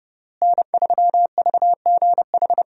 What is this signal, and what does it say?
Morse code, heard as a keyed tone, about 30 wpm, 700 Hz. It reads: N3VGH
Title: USA Call Signs - 5 Characters - Repeated 3x - 30wpm